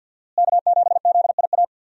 Morse code, heard as a keyed tone, about 50 wpm, 700 Hz. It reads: K7ZIA
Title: USA Call Signs - 5 Characters - Repeated 2x - 50wpm